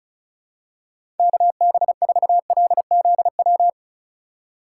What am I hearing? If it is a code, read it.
Morse code, heard as a keyed tone, about 36 wpm, 700 Hz. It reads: KB4LZW
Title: USA Call Signs - 3x Character Spacing - 35wpm